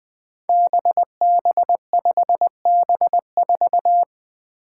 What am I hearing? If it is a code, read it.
Morse code, heard as a keyed tone, about 20 wpm, 700 Hz. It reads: BB5B4
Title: Five Challenging Letter-Number Combinations 20wpm